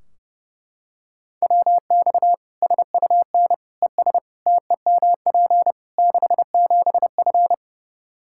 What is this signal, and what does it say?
Morse code, heard as a keyed tone, about 30 wpm, 700 Hz. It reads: WX SUN ES TEMP 67F
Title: Practice QSO - Exchange 2 - Info - 30wpm